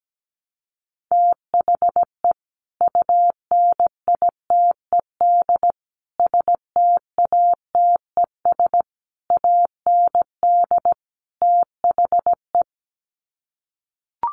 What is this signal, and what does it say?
Morse code, heard as a keyed tone, about 17 wpm, 700 Hz. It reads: THE UNITED STATES AND THE
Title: Sets of 5 Words 17wpm